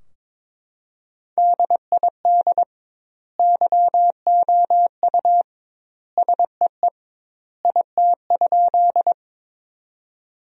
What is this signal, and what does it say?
Morse code, heard as a keyed tone, about 22 wpm, 700 Hz. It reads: DID YOU SEE IT?
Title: 2x Word-Spacing - Sentences from Top 100 Words - 22wpm